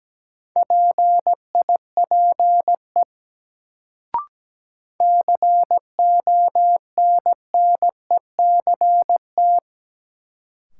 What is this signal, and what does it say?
Morse code, heard as a keyed tone, about 17 wpm, 700 Hz. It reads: PIPE CONNECT
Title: Sentences from Top 1500 Words - New Word Drill - 17wpm